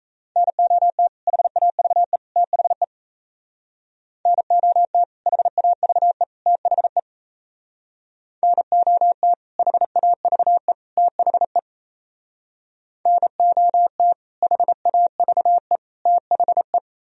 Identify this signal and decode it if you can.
Morse code, heard as a keyed tone, about 36 wpm, 700 Hz. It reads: NOT HAVE THE NOT HAVE THE NOT HAVE THE NOT HAVE THE